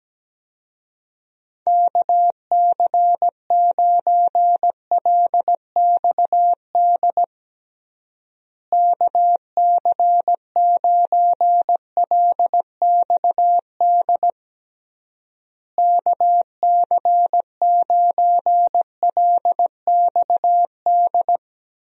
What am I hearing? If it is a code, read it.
Morse code, heard as a keyed tone, about 17 wpm, 700 Hz. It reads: KC9LXD KC9LXD KC9LXD
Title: USA Call Signs - 6 Characters - Repeated 3x - 17wpm